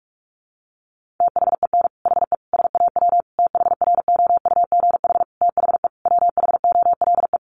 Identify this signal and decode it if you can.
Morse code, heard as a keyed tone, about 44 wpm, 700 Hz. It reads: THEN HE SAW THROUGH THE WHOLE
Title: Sentences from Top 400 Words 45wpm